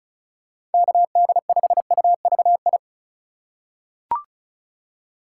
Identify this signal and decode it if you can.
Morse code, heard as a keyed tone, about 34 wpm, 700 Hz. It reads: KD5UVI
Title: USA Call Signs - 6 Characters - 2x Character Spacing - 35wpm